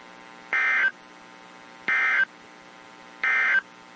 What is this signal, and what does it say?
Emergency Alert System (SAME) tones, heard as harsh buzzing bursts. It is heard from 0.4 to 3.7 s.
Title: WZ2523 - NOAA Weather Radio